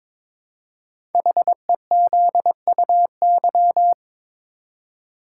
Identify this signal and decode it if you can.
Morse code, heard as a keyed tone, about 22 wpm, 700 Hz. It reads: HEZUY
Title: Code Groups - 5 Letters - 22wpm